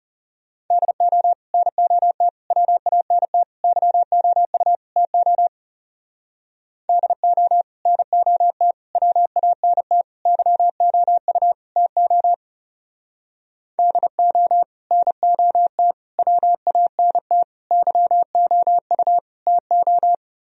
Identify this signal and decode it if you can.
Morse code, heard as a keyed tone, about 34 wpm, 700 Hz. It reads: DO NOT WANT YOU TO DO NOT WANT YOU TO DO NOT WANT YOU TO
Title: Speed-Racing - Sets of 5 Words 30wpm